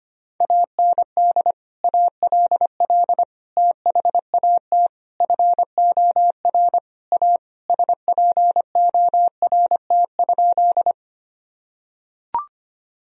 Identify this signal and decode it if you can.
Morse code, heard as a keyed tone, about 25 wpm, 700 Hz. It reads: AND ALL THAT FOR A SPORT?
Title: Sentences from Top 1000 Words - Review Part 4 - 25wpm